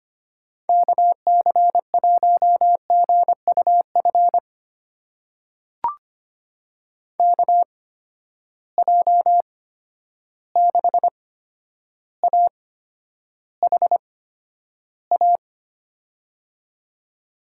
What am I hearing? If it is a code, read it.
Morse code, heard as a keyed tone, about 25 wpm, spaced as for 11 wpm, 700 Hz. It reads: KC1GUF KJ6AHA
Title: USA Call Signs - 6 Characters - 8x Character Spacing - 25wpm